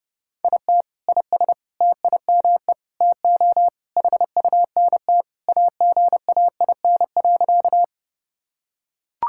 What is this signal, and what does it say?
Morse code, heard as a keyed tone, about 30 wpm, 700 Hz. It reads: IT IS TIME TO HUNT AGAIN.
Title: Sentences from Top 1600 Words 30wpm